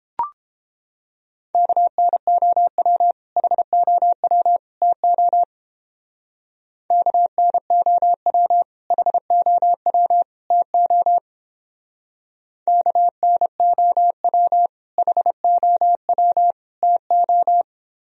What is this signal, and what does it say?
Morse code, heard as a keyed tone, about 31 wpm, 700 Hz. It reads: KNOW HOW TO KNOW HOW TO KNOW HOW TO